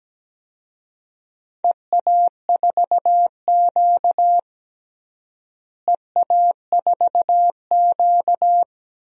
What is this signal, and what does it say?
Morse code, heard as a keyed tone, about 17 wpm, 700 Hz. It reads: EA4Q EA4Q
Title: DXCC-100 Call Signs - 4 Characters - Repeated 2x - 17wpm